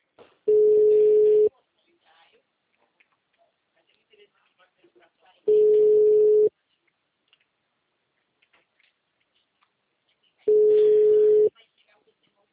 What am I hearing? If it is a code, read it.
Ringback tone (Europe, 425 Hz), with speech.
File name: out- -2018- - - .89904